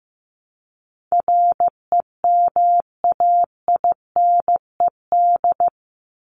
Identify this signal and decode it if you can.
Morse code, heard as a keyed tone, about 15 wpm, 700 Hz. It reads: REMAINED
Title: ED Suffix - Top 100 Words with Base Words 15wpm